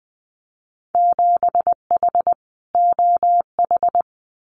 Morse code, heard as a keyed tone, about 20 wpm, 700 Hz. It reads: 7H OH
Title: Winter Field Day Exchanges Encore 20wpm